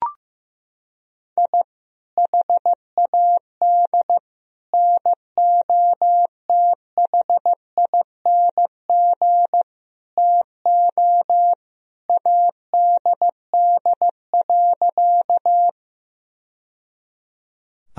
Morse code, heard as a keyed tone, about 15 wpm, 700 Hz. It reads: I HAD NOTHING TO ADD.
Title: Sentences from Top 1000 Words - Review Part 1 - 15wpm